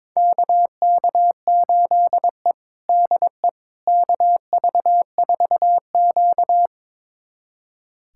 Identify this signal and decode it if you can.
Morse code, heard as a keyed tone, about 22 wpm, 700 Hz. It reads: KK8E DE KV4Q
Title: USA Callsign DE Callsign - 4 Characters - 22wpm